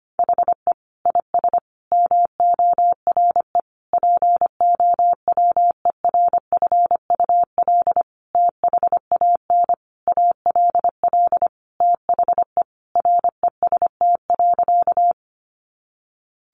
Morse code, heard as a keyed tone, about 25 wpm, 700 Hz. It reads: HE IS MORE POWERFUL THAN ALL THE REST.
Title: Sentences from Top 1300 Words 25wpm